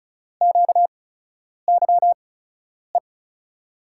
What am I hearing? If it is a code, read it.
Morse code, heard as a keyed tone, about 36 wpm, spaced as for 10 wpm, 700 Hz. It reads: QYE
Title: USA Call Signs - 6 Characters - 8x Character Spacing - 35wpm